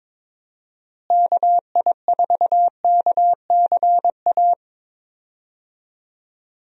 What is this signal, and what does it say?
Morse code, heard as a keyed tone, about 22 wpm, 700 Hz. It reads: KI4KCA
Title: USA Call Signs - 6 Characters - Repeated 3x - 22wpm